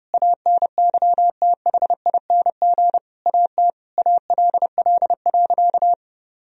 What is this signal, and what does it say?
Morse code, heard as a keyed tone, about 32 wpm, 700 Hz. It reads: ANYTHING AT ALL.
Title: Sentences from Top 1000 Words - Review Part 1 - 30wpm